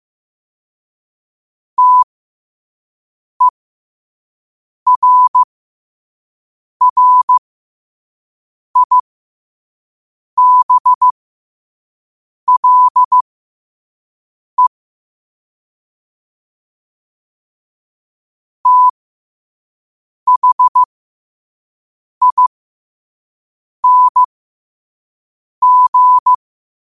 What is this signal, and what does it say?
Morse code, heard as a keyed tone, about 15 wpm, spaced as for 5 wpm, 1000 Hz. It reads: TERRIBLE THING
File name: qotdfiles / 5wpm